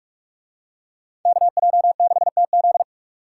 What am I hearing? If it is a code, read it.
Morse code, heard as a keyed tone, about 42 wpm, 700 Hz. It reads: KJ6TZ